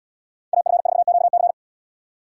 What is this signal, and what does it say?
Morse code, heard as a keyed tone, about 68 wpm, 700 Hz. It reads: AF4QC